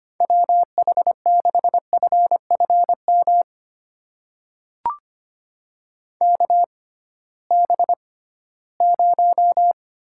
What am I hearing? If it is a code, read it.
Morse code, heard as a keyed tone, about 25 wpm, spaced as for 16 wpm, 700 Hz. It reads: WH6FFM KB0
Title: USA Call Signs - 6 Characters - 6x Character Spacing - 25wpm